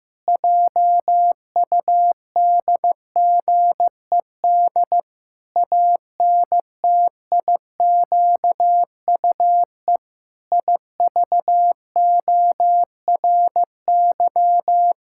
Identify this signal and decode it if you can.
Morse code, heard as a keyed tone, about 16 wpm, 700 Hz. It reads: JUDGED ANTIQUE IVORY
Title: Pangram Challenge v2 - 15wpm